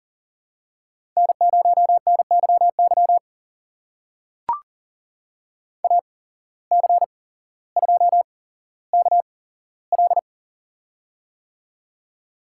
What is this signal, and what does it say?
Morse code, heard as a keyed tone, about 42 wpm, spaced as for 17 wpm, 700 Hz. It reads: N0NYY AC2KL
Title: USA Call Signs - 5 Characters - 8x Character Spacing - 40wpm